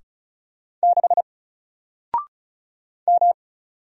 Morse code, heard as a keyed tone, about 36 wpm, 700 Hz. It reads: 6 M
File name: Single-Letter-Number-35wpm